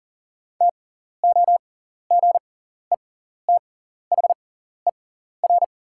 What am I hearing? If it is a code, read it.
Morse code, heard as a keyed tone, about 37 wpm, spaced as for 14 wpm, 700 Hz. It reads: TOGETHER
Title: 6x Character Spacing - Top 300 Words - 40wpm